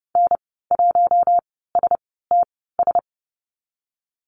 Morse code, heard as a keyed tone, about 30 wpm, spaced as for 17 wpm, 700 Hz. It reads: N1STS
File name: Lesson 14g - Number 2 - Call Signs 30wpm